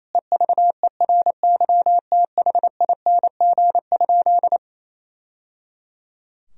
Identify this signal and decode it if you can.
Morse code, heard as a keyed tone, about 28 wpm, 700 Hz. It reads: EVERYTHING?
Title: Sentences from Top 1000 Words - Review Part 2 - 28wpm